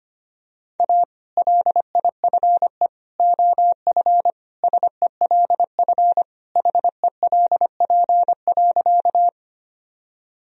Morse code, heard as a keyed tone, about 25 wpm, 700 Hz. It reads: A LIFE OF SELF HELP.
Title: Sentences from Top 1000 Words - Review Part 1 - 25wpm